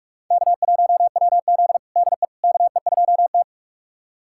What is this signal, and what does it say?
Morse code, heard as a keyed tone, about 46 wpm, 700 Hz. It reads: K1WZ DE KE2T
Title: USA Callsign DE Callsign - 4 Characters - 45wpm